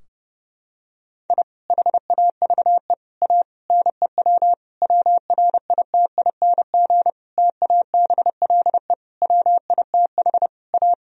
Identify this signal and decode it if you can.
Morse code, heard as a keyed tone, about 30 wpm, 700 Hz. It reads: I HAVE A NEW WRITING TABLE WITH A
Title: Sentences from Top 1000 Words - Review Part 3 - 30wpm